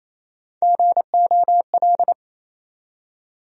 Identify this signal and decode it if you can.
Morse code, heard as a keyed tone, about 27 wpm, 700 Hz. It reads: GOL
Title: USA Call Sign Postfixes - 3 Letters 28wpm